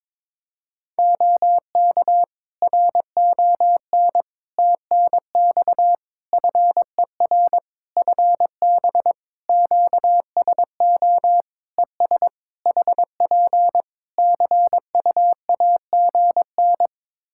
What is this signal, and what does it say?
Morse code, heard as a keyed tone, about 22 wpm, 700 Hz. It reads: OK RON TNX FER FB QSO ES HP CUAGN
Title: Practice QSO - Exchange 4 - Ending - 22wpm